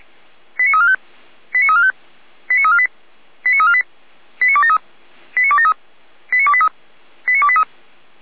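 Five-tone selective call (ZVEI), heard as quick runs of five tones, tones of about 70 ms. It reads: 89345 89345 89348 89348 89273 89273 89283 89283
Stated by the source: Five-tone ZVEI selective call received on an SDR, from the feuerwehr-orlamuende fire brigade audio server.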